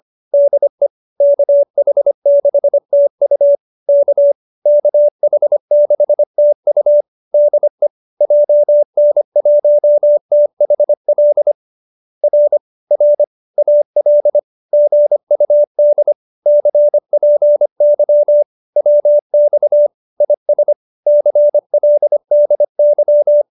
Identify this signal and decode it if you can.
Morse code, heard as a keyed tone, about 25 wpm, 590 Hz. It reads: DE KH6TU K KH6TU DE JN1THL R R AL GUD CPY WX IS CLDY